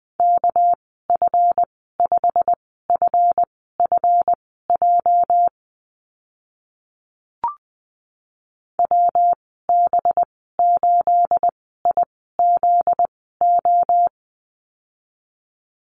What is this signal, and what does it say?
Morse code, heard as a keyed tone, about 20 wpm, spaced as for 14 wpm, 700 Hz. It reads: KF5FFJ WB8IZO
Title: 245 Lesson 39g - Letter z - Call Signs 20wpm